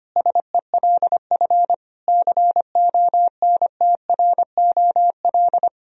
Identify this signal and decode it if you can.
Morse code, heard as a keyed tone, about 25 wpm, 700 Hz. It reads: SELF CONTROL
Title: Sentences from Top 1000 Words - Review Part 3 - 25wpm